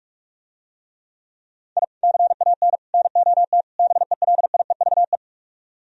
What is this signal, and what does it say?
Morse code, heard as a keyed tone, about 44 wpm, 700 Hz. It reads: I CAN NOT BELIEVE